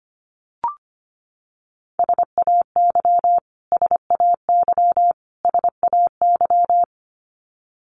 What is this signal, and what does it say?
Morse code, heard as a keyed tone, about 25 wpm, 700 Hz. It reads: SAY SAY SAY
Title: Top 500 Words Repeated 25wpm